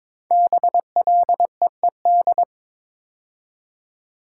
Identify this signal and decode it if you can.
Morse code, heard as a keyed tone, about 22 wpm, 700 Hz. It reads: BLEED
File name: 5 Letter Words - First 3 Letters Given - 22wpm